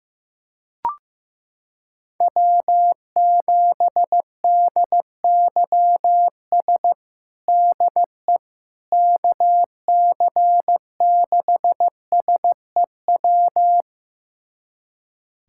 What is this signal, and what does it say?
Morse code, heard as a keyed tone, about 15 wpm, 700 Hz. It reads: W7DYS DE KC6SEW
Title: USA Callsign DE Callsign 15wpm